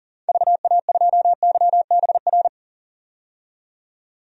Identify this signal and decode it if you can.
Morse code, heard as a keyed tone, about 40 wpm, 700 Hz. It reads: VA2YBR
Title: DXCC-100 Call Signs - 6 Characters - 40wpm